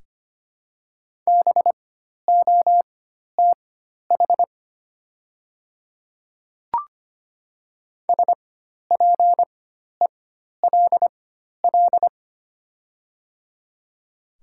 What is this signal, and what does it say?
Morse code, heard as a keyed tone, about 25 wpm, spaced as for 12 wpm, 700 Hz. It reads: BOTH SPELL
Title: 4x Character Spacing - Top 300 Words - 25wpm